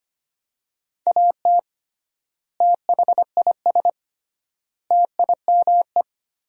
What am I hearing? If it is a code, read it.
Morse code, heard as a keyed tone, about 25 wpm, 700 Hz. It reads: AT THIS TIME
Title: 3x Word-Spacing - Sentences from Top 300 Words - 25wpm